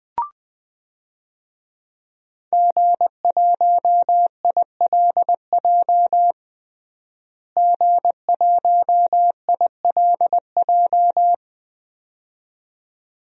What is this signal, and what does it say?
Morse code, heard as a keyed tone, about 20 wpm, 700 Hz. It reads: G1ILJ G1ILJ
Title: DXCC-100 Call Signs - 5 Characters - Repeated 2x - 20wpm